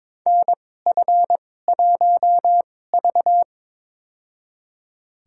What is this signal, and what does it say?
Morse code, heard as a keyed tone, about 22 wpm, spaced as for 16 wpm, 700 Hz. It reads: NF1V
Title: USA Call Signs - 2x Character Spacing - 22wpm